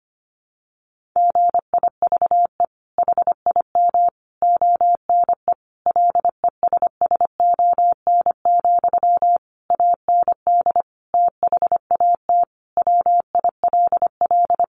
Morse code, heard as a keyed tone, about 25 wpm, 700 Hz. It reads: GIVE HIM ONE LESSON, AND THAT WILL
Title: Sentences from Top 1200 Words 25wpm